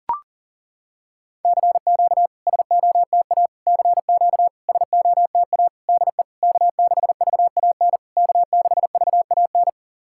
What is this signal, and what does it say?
Morse code, heard as a keyed tone, about 40 wpm, 700 Hz. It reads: CQ SOTA CQ SOTA DE K6VAN K6VAN
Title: SOTA QSO Practice 40wpm